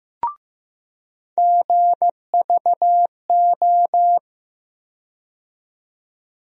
Morse code, heard as a keyed tone, about 15 wpm, 700 Hz. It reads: GVO
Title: USA Call Sign Postfixes - 3 Letters 15wpm